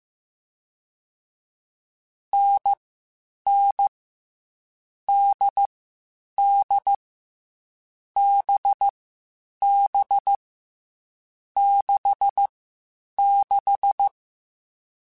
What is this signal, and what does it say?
Morse code, heard as a keyed tone, about 15 wpm, spaced as for 8 wpm, 800 Hz. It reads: NN DD BB 66
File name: Difficult Combos / Difficult Combos-mp3s / NDB NDB6 @13